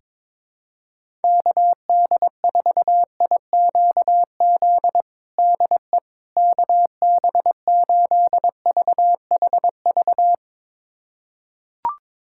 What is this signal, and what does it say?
Morse code, heard as a keyed tone, about 22 wpm, 700 Hz. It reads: KD4IQZ DE KB8VHV
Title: USA Callsign DE Callsign - 6 Characters - 22wpm